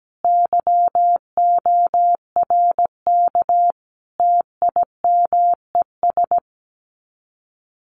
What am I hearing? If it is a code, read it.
Morse code, heard as a keyed tone, about 17 wpm, 700 Hz. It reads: YORK TIMES
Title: Sets of 5 Words 17wpm